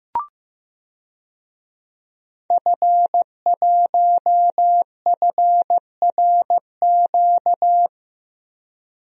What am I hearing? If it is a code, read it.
Morse code, heard as a keyed tone, about 15 wpm, 700 Hz. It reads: F1FRQ
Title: DXCC-100 Call Signs - 5 Characters - Repeated 3x - 15wpm